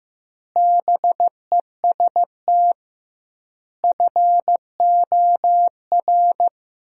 Morse code, heard as a keyed tone, about 15 wpm, 700 Hz. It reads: BEST FOR
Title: 2x Word-Spacing - Sentences from Top 500 Words - 15wpm